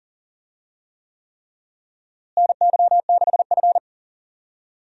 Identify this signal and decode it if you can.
Morse code, heard as a keyed tone, about 40 wpm, 700 Hz. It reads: NY6F
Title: USA Call Signs - 4 Characters - Repeated 3x - 40wpm